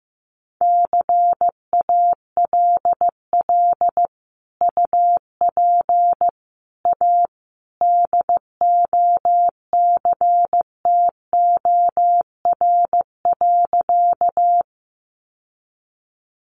Morse code, heard as a keyed tone, about 15 wpm, 700 Hz. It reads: CALL UP A DOCTOR.
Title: Sentences from Top 1500 Words 15wpm